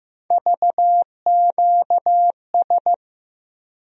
Morse code, heard as a keyed tone, about 15 wpm, 700 Hz. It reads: VQS
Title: USA Call Sign Postfixes - 3 Letters 15wpm